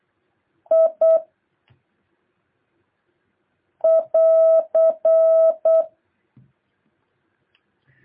Morse code, heard as a keyed tone, about 8 wpm, 650 Hz. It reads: I <AR>
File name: jh8cbh / 2023cw